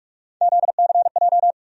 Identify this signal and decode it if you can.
Morse code, heard as a keyed tone, about 47 wpm, 700 Hz. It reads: ZCJ
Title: USA Call Sign Postfixes - 3 Letters 45wpm